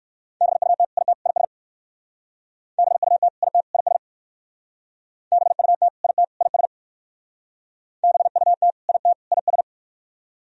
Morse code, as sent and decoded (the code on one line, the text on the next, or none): -... ..- - / .. - / .. ... / -... ..- - / .. - / .. ... / -... ..- - / .. - / .. ... / -... ..- - / .. - / .. ...
BUT IT IS BUT IT IS BUT IT IS BUT IT IS